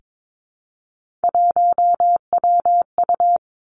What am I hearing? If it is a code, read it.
Morse code, heard as a keyed tone, about 22 wpm, 700 Hz. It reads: 1WU
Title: Three Challenging Letter-Number Combinations 22wpm